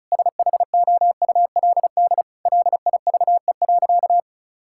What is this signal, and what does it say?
Morse code, heard as a keyed tone, about 36 wpm, 700 Hz. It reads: SHOULD LIVE.
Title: Sentences from Top 1000 Words - Review Part 5 - 35wpm